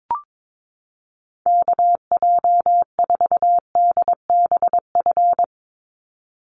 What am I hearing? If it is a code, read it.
Morse code, heard as a keyed tone, about 22 wpm, 700 Hz. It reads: KJ4DBF
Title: USA Call Signs 22wpm